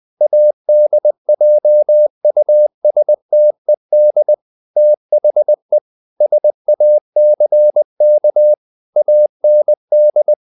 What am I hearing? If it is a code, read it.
Morse code, heard as a keyed tone, about 20 wpm, 600 Hz. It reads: ADJUSTED THE SACK AND
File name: the old man and the sea / the old man and the sea part 1 20wpm